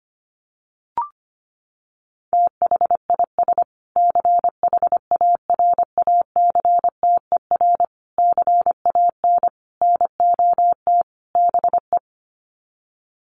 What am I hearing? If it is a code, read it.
Morse code, heard as a keyed tone, about 25 wpm, 700 Hz. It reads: THIS CHARACTER CAN NOT BE
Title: Sets of 5 Words Encore2 25wpm